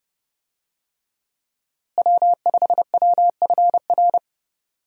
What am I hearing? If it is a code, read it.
Morse code, heard as a keyed tone, about 31 wpm, 700 Hz. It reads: W5WFR
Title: USA Call Signs - 5 Characters - Repeated 3x - 30wpm